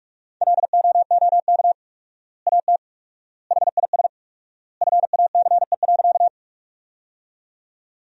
Morse code, heard as a keyed tone, about 48 wpm, 700 Hz. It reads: LOOK AT HIS FACE.
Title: 4x Word-Spacing - Sentences from Top 300 Words - 45wpm